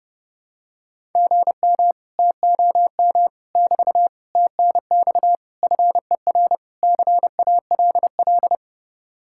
Morse code, heard as a keyed tone, about 30 wpm, 700 Hz. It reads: GM TOM <BT> TNX FER CALL